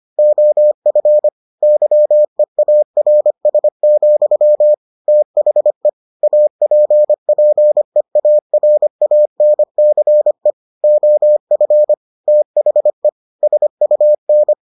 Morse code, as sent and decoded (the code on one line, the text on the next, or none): --- ..-. / -.-- . .- .-. ... --..-- / - .... / .- .--. .--. .- .-. .- -. -.-. / --- ..-. / - .... / ... ..- -.
OF YEARS, THE APPEARANCE OF THE SUN